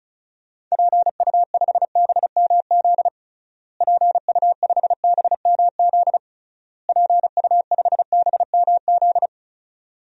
Morse code, heard as a keyed tone, about 36 wpm, 700 Hz. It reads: PU5BMZ PU5BMZ PU5BMZ
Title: DXCC-100 Call Signs - 6 Characters - Repeated 3x - 35wpm